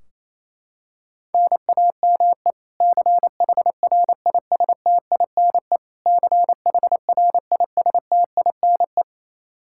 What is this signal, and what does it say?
Morse code, heard as a keyed tone, about 28 wpm, 700 Hz. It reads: NAME CHRISTINE CHRISTINE
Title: Practice QSO - Exchange 1 - Beginning - 28wpm